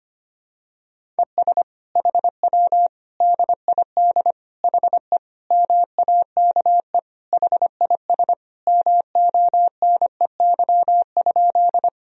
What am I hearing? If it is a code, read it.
Morse code, heard as a keyed tone, about 25 wpm, 700 Hz. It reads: ES HW DID HE MAKE HIS MONEY?
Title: Sentences from Top 400 Words and Abbreviations 25wpm